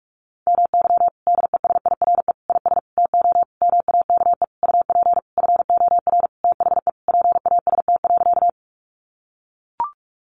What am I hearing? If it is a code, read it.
Morse code, heard as a keyed tone, about 44 wpm, 700 Hz. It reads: MY DESIRE IS TO MAKE UP FOR THE PAST.
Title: Sentences from Top 1100 Words 45wpm